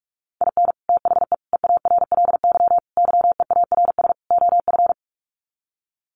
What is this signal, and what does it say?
Morse code, heard as a keyed tone, about 44 wpm, 700 Hz. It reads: IN THE EARLY YEARS OF
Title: Sets of 5 Words Encore 45wpm